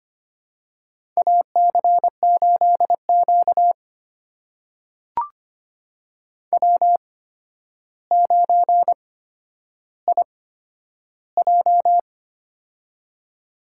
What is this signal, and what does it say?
Morse code, heard as a keyed tone, about 25 wpm, spaced as for 11 wpm, 700 Hz. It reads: AC8Q W9IJ
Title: USA Call Signs - 4 Characters - 8x Character Spacing - 25wpm